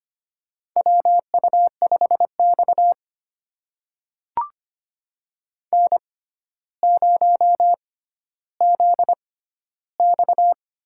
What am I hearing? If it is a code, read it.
Morse code, heard as a keyed tone, about 25 wpm, spaced as for 13 wpm, 700 Hz. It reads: WU5X N0ZX
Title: USA Call Signs - 4 Characters - 6x Character Spacing - 25wpm